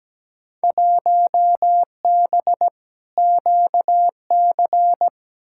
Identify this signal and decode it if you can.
Morse code, heard as a keyed tone, about 17 wpm, 700 Hz. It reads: 1B QC
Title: Field Day Exchanges - 17wpm